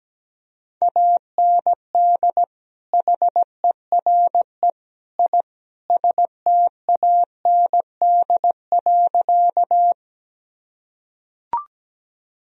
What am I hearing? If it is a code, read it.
Morse code, heard as a keyed tone, about 17 wpm, 700 Hz. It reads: AND HERE I STAND.
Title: Sentences from Top 1000 Words - Review Part 1 - 17wpm